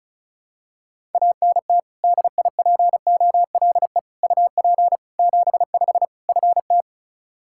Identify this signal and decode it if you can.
Morse code, heard as a keyed tone, about 36 wpm, 700 Hz. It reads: ANT DIPOLE UP 75 FT
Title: Practice QSO - Exchange 2 - Info - 35wpm